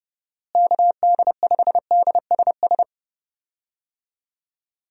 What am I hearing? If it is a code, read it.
Morse code, heard as a keyed tone, about 30 wpm, 700 Hz. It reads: KD5DSS
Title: USA Call Signs - 6 Characters - Repeated 2x - 30wpm